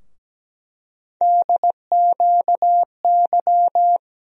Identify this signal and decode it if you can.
Morse code, heard as a keyed tone, about 17 wpm, 700 Hz. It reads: DQY
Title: USA Call Sign Postfixes - 3 Letters 17wpm